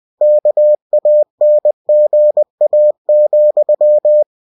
Morse code, heard as a keyed tone, about 20 wpm, 600 Hz. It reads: KANGA,